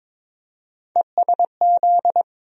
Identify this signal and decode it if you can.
Morse code, heard as a keyed tone, about 22 wpm, 700 Hz. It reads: ESZ
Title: USA Call Sign Postfixes - 3 Letters 22wpm